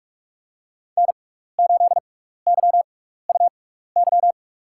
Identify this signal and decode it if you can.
Morse code, heard as a keyed tone, about 44 wpm, spaced as for 15 wpm, 700 Hz. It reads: N8YUY
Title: USA Call Signs - 5 Characters - 6x Character Spacing - 45wpm